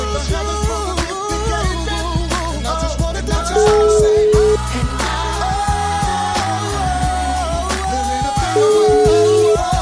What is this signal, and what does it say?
Ringback tone (Europe, 425 Hz), with noise.